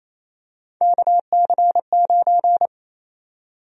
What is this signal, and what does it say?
Morse code, heard as a keyed tone, about 28 wpm, 700 Hz. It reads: KC9
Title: USA Call Sign Prefixes - Repeated 2x - 28wpm